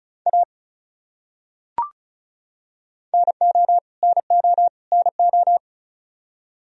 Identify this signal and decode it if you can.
Morse code, heard as a keyed tone, about 36 wpm, 700 Hz. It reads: A NO NO NO
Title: Top 100 Words Repeated 35wpm